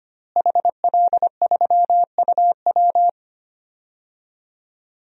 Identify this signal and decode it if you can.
Morse code, heard as a keyed tone, about 25 wpm, 700 Hz. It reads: HL3UW
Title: DXCC-100 Call Signs - 5 Characters - Repeated 2x - 25wpm